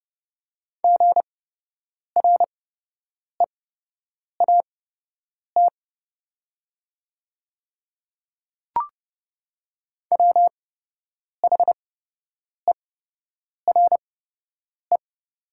Morse code, heard as a keyed tone, about 30 wpm, spaced as for 8 wpm, 700 Hz. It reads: GREAT WHERE
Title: 8x Character Spacing - Top 300 Words - 30wpm